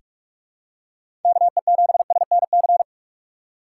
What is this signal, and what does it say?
Morse code, heard as a keyed tone, about 47 wpm, 700 Hz. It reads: KE7SNC
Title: USA Call Signs - 4x Character Spacing - 45wpm